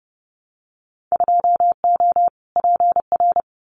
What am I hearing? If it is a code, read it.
Morse code, heard as a keyed tone, about 31 wpm, 700 Hz. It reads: 2O PR
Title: Winter Field Day Exchanges Encore 30wpm